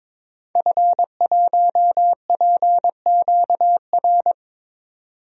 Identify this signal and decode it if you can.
Morse code, heard as a keyed tone, about 22 wpm, 700 Hz. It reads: F1PQR
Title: DXCC-100 Call Signs - 5 Characters - 22wpm